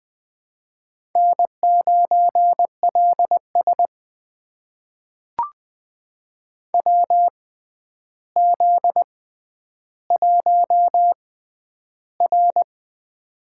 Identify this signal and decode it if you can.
Morse code, heard as a keyed tone, about 20 wpm, spaced as for 10 wpm, 700 Hz. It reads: N9LS WZ1R
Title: USA Call Signs - 6x Character Spacing - 20wpm